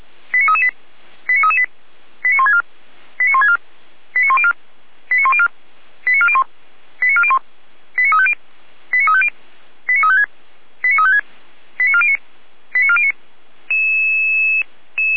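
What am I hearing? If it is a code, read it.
Five-tone selective call (ZVEI), heard as quick runs of five tones, tones of about 70 ms. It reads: 89338 89338 89164 89164 89104 89104 89401 89401 89350 89350 89356 89356 89409 89409
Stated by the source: ZVEI five-tone selective call received on SDR.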